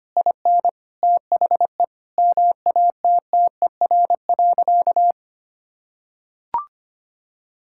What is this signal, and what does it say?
Morse code, heard as a keyed tone, about 25 wpm, 700 Hz. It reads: IN THE MATTER.
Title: Sentences from Top 1000 Words - Review Part 5 - 25wpm